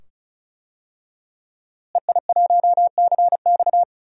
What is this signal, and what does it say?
Morse code, heard as a keyed tone, about 36 wpm, 700 Hz. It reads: EI1CX
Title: DXCC-100 Call Signs - 5 Characters - Repeated 2x - 35wpm